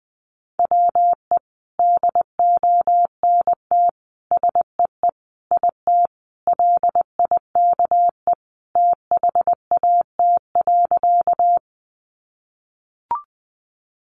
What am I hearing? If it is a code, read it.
Morse code, heard as a keyed tone, about 20 wpm, 700 Hz. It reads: WE DONT SEE IT LIKE THAT.
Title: Sentences from Top 300 Words 20wpm